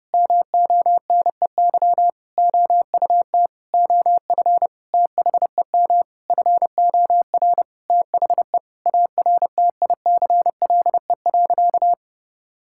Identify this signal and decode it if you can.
Morse code, heard as a keyed tone, about 30 wpm, 700 Hz. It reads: MONEY OUT OF THEM FOR THE ARTICLE.
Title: Sentences from Top 1000 Words - Review Part 5 - 30wpm